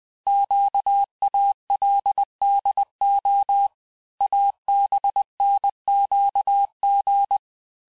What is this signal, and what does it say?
Morse code, heard as a keyed tone, about 20 wpm, 800 Hz. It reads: QALDO ABNQG
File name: cours-1584-teste-chiffre-vitesse-19mpm